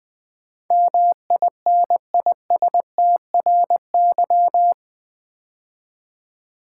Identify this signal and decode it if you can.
Morse code, heard as a keyed tone, about 20 wpm, 700 Hz. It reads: MINISTRY
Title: 8 Letter Words - First 3 Letters Given - 20wpm